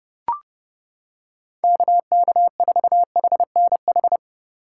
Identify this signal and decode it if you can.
Morse code, heard as a keyed tone, about 30 wpm, 700 Hz. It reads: KK4HNH